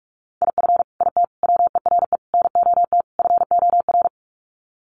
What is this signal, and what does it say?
Morse code, heard as a keyed tone, about 44 wpm, 700 Hz. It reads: IF IT WERE NOT FOR
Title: Sets of 5 Words 45wpm